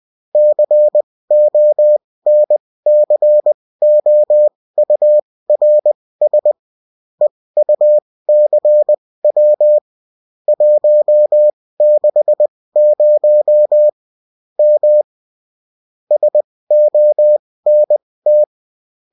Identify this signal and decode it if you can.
Morse code, heard as a keyed tone, about 20 wpm, spaced as for 16 wpm, 600 Hz. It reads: CONCOURS EUCW 160 M SONT